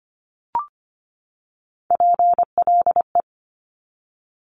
Morse code, heard as a keyed tone, about 25 wpm, 700 Hz. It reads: PLE
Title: Top Three Letter Combinations in Words 25wpm